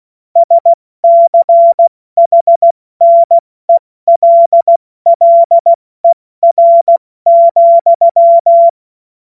Morse code, heard as a keyed tone, about 16 wpm, 680 Hz. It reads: SCHNELLER,
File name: Morsekurs / Witze / Witz 20. Januar 2026 70 F80